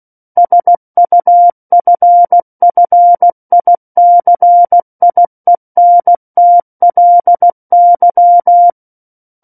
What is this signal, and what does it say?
Morse code, heard as a keyed tone, about 16 wpm, 700 Hz. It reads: SUFFICIENTLY
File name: Book One / 07-Chapter 7 - 16 WPM